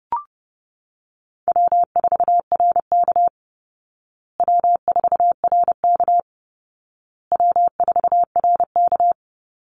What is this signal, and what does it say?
Morse code, heard as a keyed tone, about 30 wpm, 700 Hz. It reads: W4RK W4RK W4RK
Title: Call Signs Repeated 2x - 30wpm